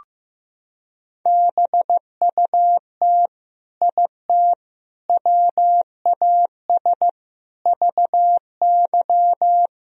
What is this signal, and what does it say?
Morse code, heard as a keyed tone, about 15 wpm, 700 Hz. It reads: BUT IT WAS VY
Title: Sentences from Top 300 Words and Abbreviations 15wpm